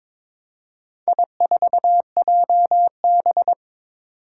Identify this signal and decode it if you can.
Morse code, heard as a keyed tone, about 23 wpm, 700 Hz. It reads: I4JB
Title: Four Challenging Letter-Number Combinations 22wpm